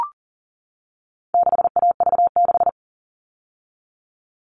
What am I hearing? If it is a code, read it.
Morse code, heard as a keyed tone, about 40 wpm, 700 Hz. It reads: 6AV6
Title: All Cut Numbers - 4-Digit 40wpm